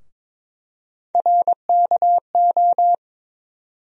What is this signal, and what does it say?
Morse code, heard as a keyed tone, about 22 wpm, 700 Hz. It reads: RKO
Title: USA Call Sign Postfixes - 3 Letters 22wpm